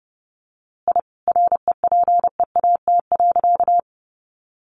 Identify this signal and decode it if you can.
Morse code, heard as a keyed tone, about 30 wpm, 700 Hz. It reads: I REPEAT.
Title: A Sound Of Thunder 30wpm